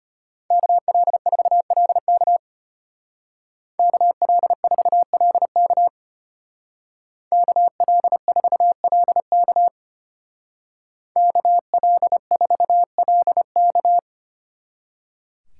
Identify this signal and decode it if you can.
Morse code, heard as a keyed tone, about 30 wpm, 700 Hz. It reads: KL4LK KL4LK KL4LK KL4LK
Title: Speed Racing - USA Call Signs 25wpm